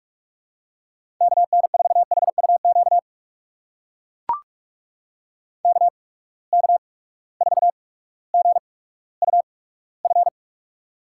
Morse code, heard as a keyed tone, about 42 wpm, spaced as for 19 wpm, 700 Hz. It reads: KN4HUQ KK4GUF
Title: USA Call Signs - 6 Characters - 8x Character Spacing - 45wpm